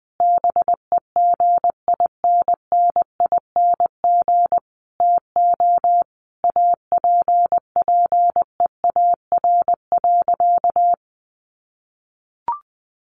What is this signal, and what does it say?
Morse code, heard as a keyed tone, about 20 wpm, 700 Hz. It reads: BEGINNING TO APPEAR.
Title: Sentences from Top 1200 Words 20wpm